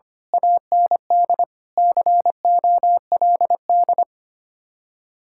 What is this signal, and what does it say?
Morse code, heard as a keyed tone, about 25 wpm, 700 Hz. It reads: AND COLD
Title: Speed-Racing Common QSO Phrases Encore - 25wpm